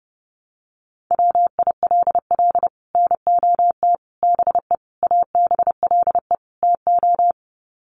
Morse code, heard as a keyed tone, about 30 wpm, 700 Hz. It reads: WILL NOT BE ABLE TO
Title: Sets of 5 Words 30wpm